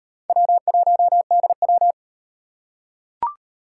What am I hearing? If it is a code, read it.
Morse code, heard as a keyed tone, about 38 wpm, 700 Hz. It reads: W1DW